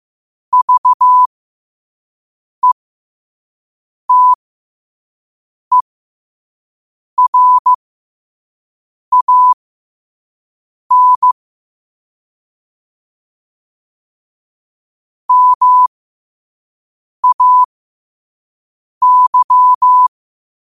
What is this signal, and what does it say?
Morse code, heard as a keyed tone, about 15 wpm, spaced as for 5 wpm, 1000 Hz. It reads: VETERAN MAY
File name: qotdfiles / 5wpm